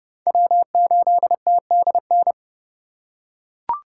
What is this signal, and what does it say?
Morse code, heard as a keyed tone, about 27 wpm, 700 Hz. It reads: W8TDN